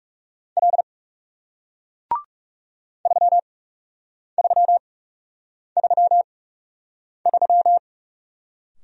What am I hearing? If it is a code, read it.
Morse code, heard as a keyed tone, about 41 wpm, spaced as for 8 wpm, 700 Hz. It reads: L 3333